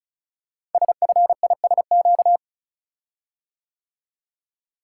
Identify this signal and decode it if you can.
Morse code, heard as a keyed tone, about 36 wpm, 700 Hz. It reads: SFISQ
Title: Code Groups - 5 Letter:Numbers - 35wpm